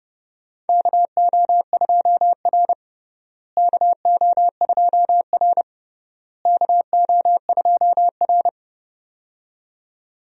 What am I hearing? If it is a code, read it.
Morse code, heard as a keyed tone, about 30 wpm, 700 Hz. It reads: KO2R KO2R KO2R
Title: USA Call Signs - 4 Characters - Repeated 3x - 30wpm